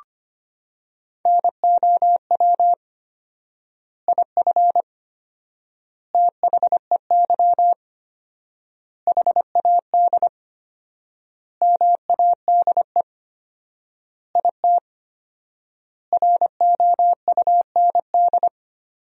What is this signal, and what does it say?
Morse code, heard as a keyed tone, about 25 wpm, 700 Hz. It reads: NOW IF THEY HAD MADE IT ROUND
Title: 4x Word-Spacing - Sentences from Top 500 Words - 25wpm